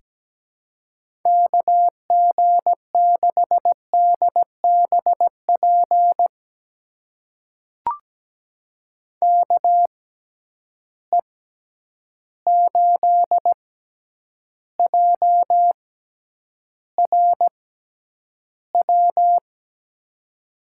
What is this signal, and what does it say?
Morse code, heard as a keyed tone, about 17 wpm, spaced as for 9 wpm, 700 Hz. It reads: KG6DBP KE8JRW
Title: USA Call Signs - 6x Character Spacing - 17wpm